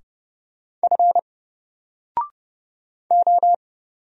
Morse code, heard as a keyed tone, about 30 wpm, 700 Hz. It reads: F O
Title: Single Character (Letter:Number:Punctuation:Prosign) 30wpm